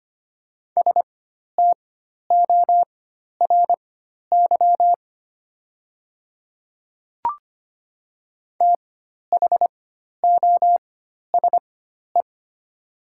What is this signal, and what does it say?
Morse code, heard as a keyed tone, about 25 wpm, spaced as for 12 wpm, 700 Hz. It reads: STORY THOSE
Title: 4x Character Spacing - Top 300 Words - 25wpm